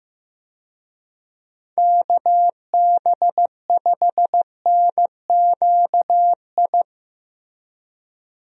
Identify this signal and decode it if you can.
Morse code, heard as a keyed tone, about 16 wpm, 700 Hz. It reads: KB5NQI
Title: USA Call Signs - 6 Characters - Repeated 2x - 15wpm